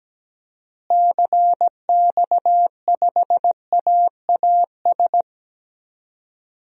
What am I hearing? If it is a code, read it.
Morse code, heard as a keyed tone, about 17 wpm, 700 Hz. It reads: CX5AAS
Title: DXCC-100 Call Signs - 6 Characters - Repeated 3x - 17wpm